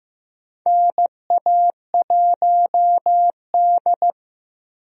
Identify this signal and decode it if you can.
Morse code, heard as a keyed tone, about 15 wpm, 700 Hz. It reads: NA1D
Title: USA Call Signs - 4 Characters - Repeated 3x - 15wpm